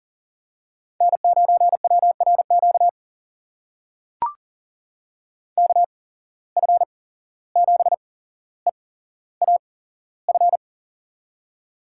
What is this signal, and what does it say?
Morse code, heard as a keyed tone, about 40 wpm, spaced as for 16 wpm, 700 Hz. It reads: N9WRQ KF7EAF
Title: USA Call Signs - 8x Character Spacing - 40wpm